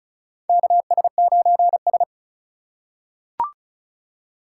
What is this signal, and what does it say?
Morse code, heard as a keyed tone, about 36 wpm, 700 Hz. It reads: KS9S
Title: USA Call Signs - 4 Characters - 4x Character Spacing - 35wpm